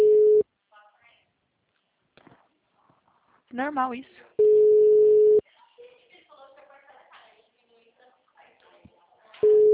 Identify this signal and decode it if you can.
Ringback tone (Europe, 425 Hz), with speech.
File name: out- -2006- - - .21796